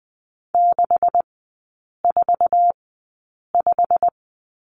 Morse code, heard as a keyed tone, about 20 wpm, spaced as for 8 wpm, 700 Hz. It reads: 645
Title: 127 Lesson 21e - Number 6 - Three Characters 20wpm